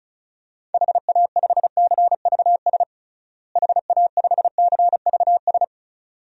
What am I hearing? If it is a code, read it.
Morse code, heard as a keyed tone, about 36 wpm, 700 Hz. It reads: HA5CVS HA5CVS
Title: DXCC-100 Call Signs - 6 Characters - Repeated 2x - 35wpm